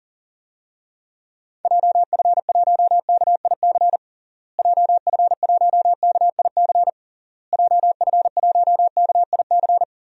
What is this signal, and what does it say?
Morse code, heard as a keyed tone, about 40 wpm, 700 Hz. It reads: JF1KIC JF1KIC JF1KIC
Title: DXCC-100 Call Signs - 6 Characters - Repeated 3x - 40wpm